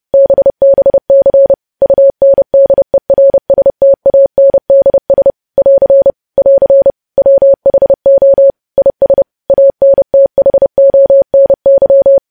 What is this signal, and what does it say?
Morse code, heard as a keyed tone, about 30 wpm, 550 Hz. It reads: BBC UNDERSTANDS <AR> <AR> WHO IS ANTHONY
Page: news headlines in morse code at 30 wpm